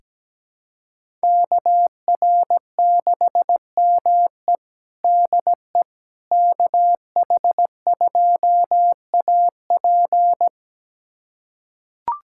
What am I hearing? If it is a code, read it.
Morse code, heard as a keyed tone, about 17 wpm, 700 Hz. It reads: KR6ME DE KH2AP
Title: USA Callsign DE Callsign - 5 Characters - 17wpm